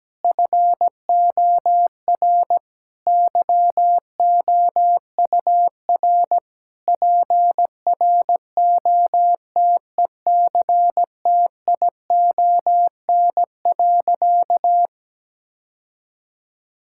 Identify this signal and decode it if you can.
Morse code, heard as a keyed tone, about 17 wpm, 700 Hz. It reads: FOR YOUR PROTECTION.
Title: Sentences from Top 1600 Words 17wpm